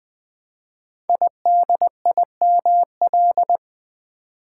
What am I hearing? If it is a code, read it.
Morse code, heard as a keyed tone, about 20 wpm, 700 Hz. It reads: IDIML
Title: Code Groups - 5 Letter:Numbers - 20wpm